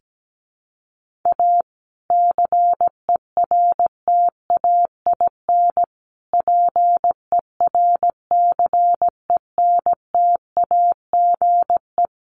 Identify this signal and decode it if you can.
Morse code, heard as a keyed tone, about 17 wpm, 700 Hz. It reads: A CERTAIN PERCENTAGE
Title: Sentences from Top 1300 Words 17wpm